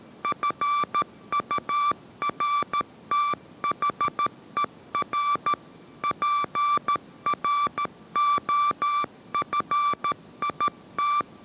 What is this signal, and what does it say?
Morse code, heard as a keyed tone, about 13 wpm, 1170 Hz. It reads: FURTHER PROFIT